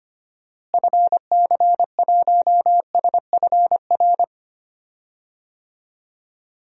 Morse code, heard as a keyed tone, about 25 wpm, 700 Hz. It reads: FC1SFR
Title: DXCC-100 Call Signs - 6 Characters - Repeated 3x - 25wpm